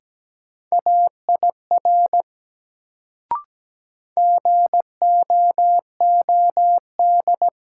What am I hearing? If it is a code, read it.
Morse code, heard as a keyed tone, about 17 wpm, 700 Hz. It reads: AIR GOOD
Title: Sentences from Top 200 Words - New Word Drill - 17wpm